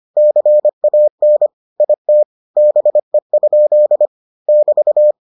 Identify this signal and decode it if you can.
Morse code, heard as a keyed tone, about 25 wpm, 600 Hz. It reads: CAN IT BE? <BT>